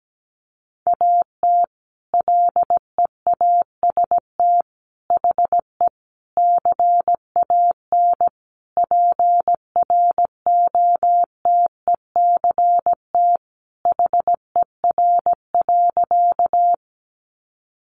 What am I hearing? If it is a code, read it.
Morse code, heard as a keyed tone, about 17 wpm, 700 Hz. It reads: AT LEAST HE CAN PROTECT HER.
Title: Sentences from Top 900 Words 17wpm